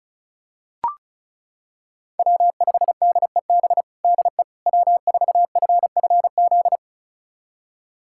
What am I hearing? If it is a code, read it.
Morse code, heard as a keyed tone, about 36 wpm, 700 Hz. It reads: W5DEB DE W4FFZ
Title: USA Callsign DE Callsign - 5 Characters - 35wpm